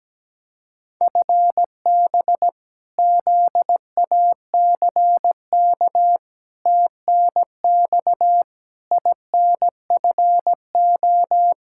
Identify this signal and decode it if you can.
Morse code, heard as a keyed tone, about 17 wpm, 700 Hz. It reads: FB ZACK TNX INFO
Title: Practice QSO - Exchange 3 - More Info - 17wpm